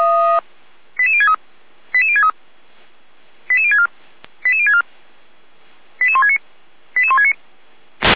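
Five-tone selective call (ZVEI), heard as quick runs of five tones, tones of about 70 ms. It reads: 88063 88063 88064 88064 88169 88169
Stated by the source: ZVEI five-tone selective call 88352, received by SDR.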